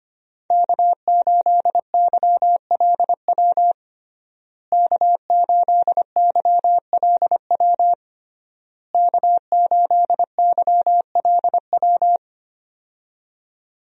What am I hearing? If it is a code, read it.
Morse code, heard as a keyed tone, about 25 wpm, 700 Hz. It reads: K8YLW K8YLW K8YLW
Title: USA Call Signs - 5 Characters - Repeated 3x - 25wpm